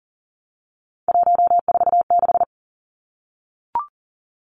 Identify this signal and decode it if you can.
Morse code, heard as a keyed tone, about 40 wpm, 700 Hz. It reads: 146
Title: Numbers - 3-Digit 40wpm